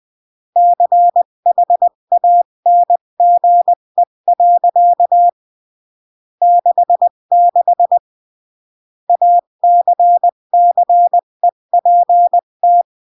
Morse code, heard as a keyed tone, about 20 wpm, 700 Hz. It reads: CHANGE. 66 ACCEPT
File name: CW / jodorowsky / 82 commandments 20-15